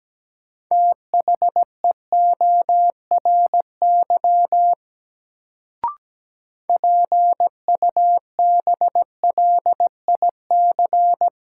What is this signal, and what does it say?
Morse code, heard as a keyed tone, about 17 wpm, 700 Hz. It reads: THEORY PUBLIC
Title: Sentences from Top 600 Words - New Word Drill - 17wpm